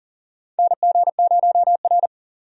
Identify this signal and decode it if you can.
Morse code, heard as a keyed tone, about 40 wpm, 700 Hz. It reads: NG0R
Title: USA Call Signs - 4 Characters - 40wpm